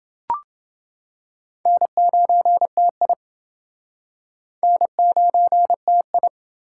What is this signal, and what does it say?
Morse code, heard as a keyed tone, about 29 wpm, 700 Hz. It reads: N9TI N9TI